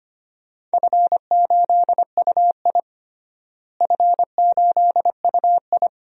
Morse code, heard as a keyed tone, about 25 wpm, 700 Hz. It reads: F8UI F8UI
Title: DXCC-100 Call Signs - 4 Characters - Repeated 2x - 25wpm